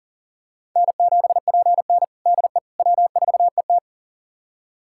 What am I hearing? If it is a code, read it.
Morse code, heard as a keyed tone, about 40 wpm, 700 Hz. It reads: N7PN DE W4ET
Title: USA Callsign DE Callsign - 4 Characters - 40wpm